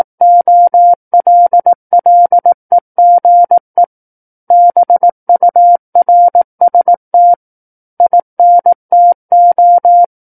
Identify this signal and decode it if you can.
Morse code, heard as a keyed tone, about 18 wpm, 700 Hz. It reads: OLLEGE BURST INTO
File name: Book One / 09-Chapter 9 - 18 WPM